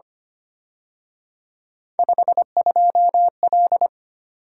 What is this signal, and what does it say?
Morse code, heard as a keyed tone, about 29 wpm, 700 Hz. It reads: 52L
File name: Speed-Racing - Three Letter-Number Combinations - 25wpm